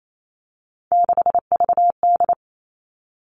Morse code, heard as a keyed tone, about 28 wpm, 700 Hz. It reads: 6VD
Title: All Cut Numbers - 3-Digit 28wpm